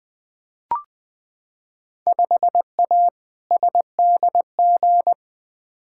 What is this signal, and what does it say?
Morse code, heard as a keyed tone, about 20 wpm, 700 Hz. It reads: 5A SDG
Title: Field Day Exchanges Encore - 20wpm